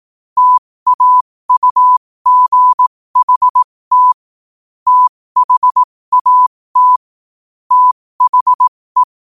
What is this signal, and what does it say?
Morse code, heard as a keyed tone, about 18 wpm, spaced as for 15 wpm, 1000 Hz. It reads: TAUGHT THAT THE
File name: qotdfiles / 15wpm